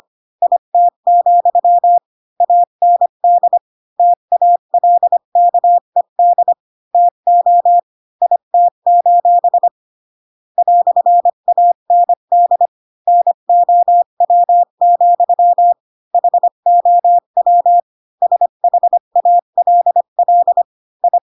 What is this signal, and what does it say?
Morse code, heard as a keyed tone, about 25 wpm, 700 Hz. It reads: IT, AND TALKED TO IT: "AND NOW, HOW SHALL I
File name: Pinocchio / Pinocchio ch5 20 25 ik1hge